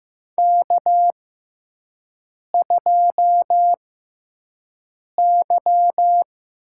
Morse code, heard as a keyed tone, about 15 wpm, spaced as for 5 wpm, 700 Hz. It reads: K2Y
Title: USA Call Signs - 4 Characters - 6x Character Spacing - 15wpm